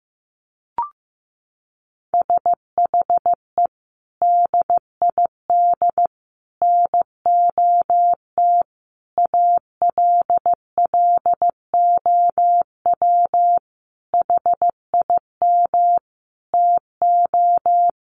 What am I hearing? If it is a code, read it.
Morse code, heard as a keyed tone, about 15 wpm, 700 Hz. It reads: SHE DID NOT ALLOW HIM TO
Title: Sentences from Top 1200 Words 15wpm